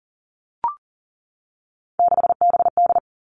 Morse code, heard as a keyed tone, about 38 wpm, 700 Hz. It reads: 6BD
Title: Three Challenging Letter-Number Combinations 40wpm